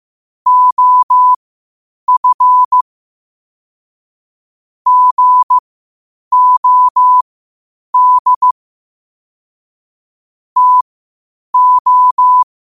Morse code, heard as a keyed tone, about 15 wpm, spaced as for 8 wpm, 1000 Hz. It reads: OF GOD TO
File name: qotdfiles / 8wpm